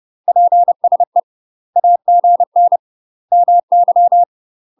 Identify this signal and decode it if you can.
Morse code, heard as a keyed tone, about 30 wpm, 700 Hz. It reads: PSE AGN MY
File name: UK MASTER QSOs TRAINING / 30WPM / UK MASTER QSOs 0017